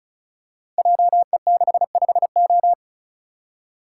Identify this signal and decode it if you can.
Morse code, heard as a keyed tone, about 36 wpm, 700 Hz. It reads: JE65O
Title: Code Groups - 5 Letter:Numbers - 35wpm